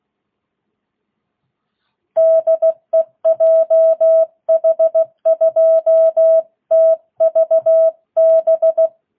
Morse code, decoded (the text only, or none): DE JH2TVB